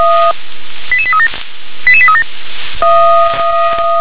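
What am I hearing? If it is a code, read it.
Five-tone selective call (ZVEI), heard as quick runs of five tones, tones of about 70 ms. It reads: 88837 88837
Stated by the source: ZVEI five-tone selective call 88743 received on SDR.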